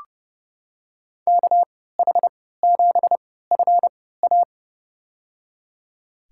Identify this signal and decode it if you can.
Morse code, heard as a keyed tone, about 30 wpm, spaced as for 17 wpm, 700 Hz. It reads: KH7FA
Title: USA Call Signs - 5 Characters - 3x Character Spacing - 30wpm